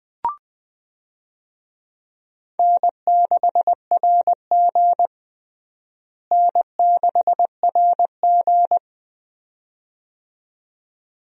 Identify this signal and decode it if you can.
Morse code, heard as a keyed tone, about 20 wpm, 700 Hz. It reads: N6RG N6RG
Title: USA Call Signs - 4 Characters - Repeated 2x - 20wpm